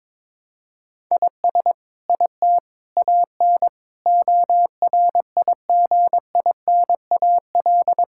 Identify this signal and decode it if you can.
Morse code, heard as a keyed tone, about 22 wpm, 700 Hz. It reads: IS IT AN ORIGINAL
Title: Sentences from Top 1000 Words - Review Part 6 - 22wpm